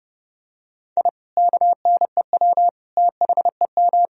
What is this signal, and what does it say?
Morse code, heard as a keyed tone, about 30 wpm, 700 Hz. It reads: I KNEW THEM
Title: Sentences from Top 1000 Words - Review Part 1 - 30wpm